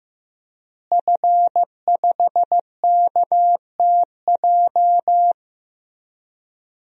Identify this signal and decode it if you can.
Morse code, heard as a keyed tone, about 15 wpm, 700 Hz. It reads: F5KTJ
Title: DXCC-100 Call Signs - 5 Characters - Repeated 3x - 15wpm